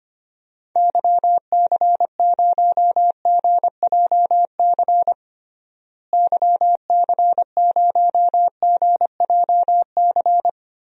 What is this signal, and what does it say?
Morse code, heard as a keyed tone, about 25 wpm, 700 Hz. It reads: YC0GJC YC0GJC
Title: DXCC-100 Call Signs - 6 Characters - Repeated 3x - 25wpm